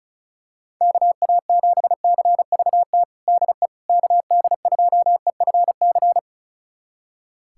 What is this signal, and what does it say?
Morse code, heard as a keyed tone, about 36 wpm, 700 Hz. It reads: KA7CVT DE KD2EFC
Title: USA Callsign DE Callsign 35wpm